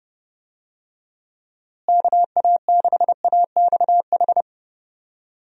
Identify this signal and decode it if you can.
Morse code, heard as a keyed tone, about 32 wpm, 700 Hz. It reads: KA6AXH
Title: USA Call Signs - 6 Characters - Repeated 3x - 30wpm